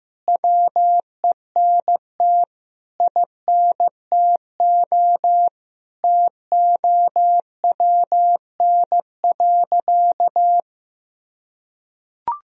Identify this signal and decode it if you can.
Morse code, heard as a keyed tone, about 15 wpm, 700 Hz. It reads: WENT INTO TOWN.
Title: Sentences from Top 1000 Words - Review Part 1 - 15wpm